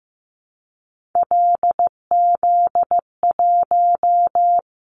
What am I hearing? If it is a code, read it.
Morse code, heard as a keyed tone, about 15 wpm, 700 Hz. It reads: LZ1
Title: Callsign Prefixes for Top 25 DXCC - Repeated 3x - 15wpm